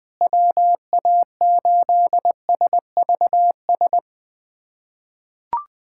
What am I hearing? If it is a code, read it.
Morse code, heard as a keyed tone, about 20 wpm, 700 Hz. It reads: WA8SVS
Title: USA Callsign DE Callsign 20wpm